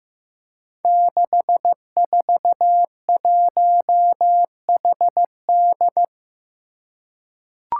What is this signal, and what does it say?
Morse code, heard as a keyed tone, about 15 wpm, 700 Hz. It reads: 641HD
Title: Five Challenging Letter-Number Combinations 15wpm